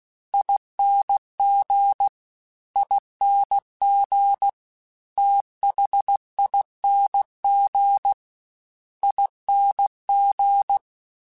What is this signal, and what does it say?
Morse code, heard as a keyed tone, about 16 wpm, 800 Hz. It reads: ING ING THING ING
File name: Special INGs-mp3s / Special INGs-mp3s / Fast ing @15